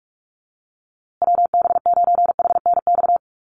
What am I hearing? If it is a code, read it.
Morse code, heard as a keyed tone, about 44 wpm, 700 Hz. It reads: WB9HNX
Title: USA Call Signs Repeated 3x 45wpm